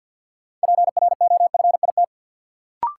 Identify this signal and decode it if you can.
Morse code, heard as a keyed tone, about 53 wpm, 700 Hz. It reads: PROFIT
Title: Sentences from Top 800 Words - New Word Drill - 50wpm